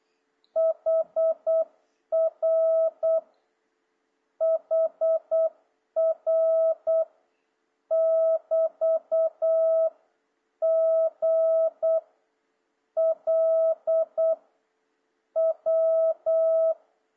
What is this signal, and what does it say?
Morse code, heard as a keyed tone, about 8 wpm, 650 Hz. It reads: HR HR <BT> G L W
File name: jh8cbh / 2026cw